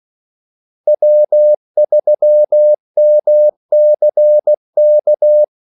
Morse code, heard as a keyed tone, about 16 wpm, 600 Hz. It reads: W3MCK
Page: Kaldesignaler 16 wpm